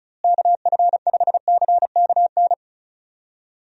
Morse code, heard as a keyed tone, about 34 wpm, 700 Hz. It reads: KF5CKN
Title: USA Call Signs - 6 Characters - Repeated 2x - 35wpm